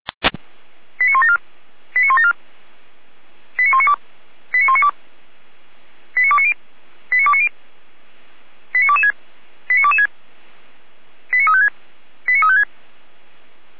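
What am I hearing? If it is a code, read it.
Five-tone selective call (ZVEI), heard as quick runs of five tones, tones of about 70 ms. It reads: 89174 89174 89192 89192 89290 89290 89226 89226 89356 89356
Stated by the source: Five-tone ZVEI selective call received by SDR.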